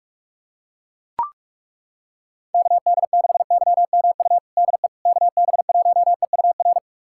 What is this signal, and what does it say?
Morse code, heard as a keyed tone, about 42 wpm, 700 Hz. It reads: KD6YMU DE KB1EUR
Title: USA Callsign DE Callsign - 6 Characters - 45wpm